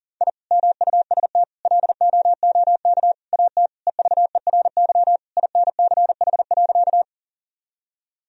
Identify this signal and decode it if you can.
Morse code, heard as a keyed tone, about 37 wpm, 700 Hz. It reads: I MUST LOOK AT EVERY INCH.
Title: Sentences from Top 1000 Words - Review Part 3 - 40wpm